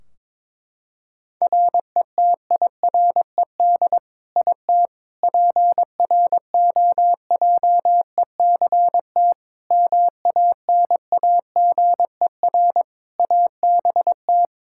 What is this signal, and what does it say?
Morse code, heard as a keyed tone, about 21 wpm, 700 Hz. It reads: RETIRED IT PROJECT MANAGER ABT
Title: Practice QSO - Exchange 3 - More Info - 22wpm